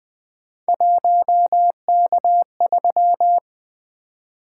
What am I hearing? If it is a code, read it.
Morse code, heard as a keyed tone, about 20 wpm, 700 Hz. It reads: 1K3
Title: Three Letter-Number Combinations - 20wpm